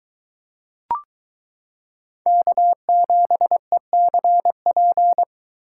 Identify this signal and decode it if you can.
Morse code, heard as a keyed tone, about 21 wpm, 700 Hz. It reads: K7ECP